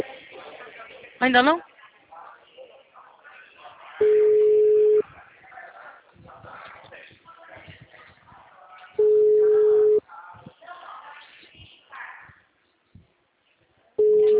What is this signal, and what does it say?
Ringback tone (Europe, 425 Hz), with speech.
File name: out- -2004- - - .15953